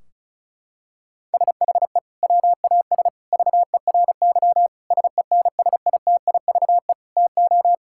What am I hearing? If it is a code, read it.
Morse code, heard as a keyed tone, about 34 wpm, 700 Hz. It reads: SHE WAS VERY SENSITIVE TO
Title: Sentences from Top 1600 Words 35wpm